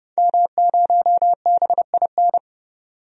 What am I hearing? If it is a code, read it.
Morse code, heard as a keyed tone, about 30 wpm, 700 Hz. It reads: M0BIN
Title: DXCC-100 Call Signs - 5 Characters - 30wpm